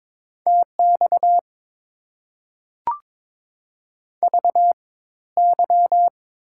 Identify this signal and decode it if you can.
Morse code, heard as a keyed tone, about 21 wpm, spaced as for 14 wpm, 690 Hz. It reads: TX VY
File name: Top 90 QSO Elements - 4x Character Spacing - 22wpm